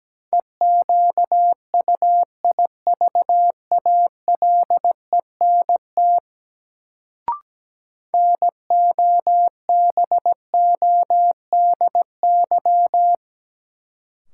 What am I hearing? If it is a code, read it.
Morse code, heard as a keyed tone, about 17 wpm, 700 Hz. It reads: EQUIVALENT NOBODY
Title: Sentences from Top 1600 Words - New Word Drill Alt - 17wpm